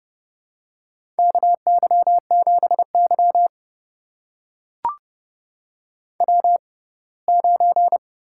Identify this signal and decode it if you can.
Morse code, heard as a keyed tone, about 28 wpm, spaced as for 20 wpm, 700 Hz. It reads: KY7Y W9
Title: USA Call Signs - 4 Characters - 6x Character Spacing - 30wpm